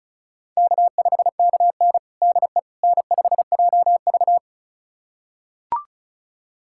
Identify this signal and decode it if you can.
Morse code, heard as a keyed tone, about 36 wpm, 700 Hz. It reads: K5KN DE N5JV
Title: USA Callsign DE Callsign - 4 Characters - 35wpm